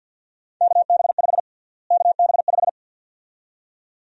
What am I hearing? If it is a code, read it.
Morse code, heard as a keyed tone, about 53 wpm, 700 Hz. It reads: KB5 KB5
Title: USA Call Sign Prefixes - Repeated 2x - 50wpm